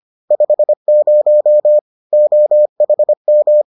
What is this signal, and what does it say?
Morse code, heard as a keyed tone, about 25 wpm, 600 Hz. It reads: 50 OHM